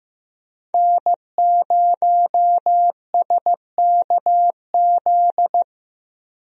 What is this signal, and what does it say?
Morse code, heard as a keyed tone, about 15 wpm, 700 Hz. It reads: N0SKZ
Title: USA Call Signs - 5 Characters - Repeated 3x - 15wpm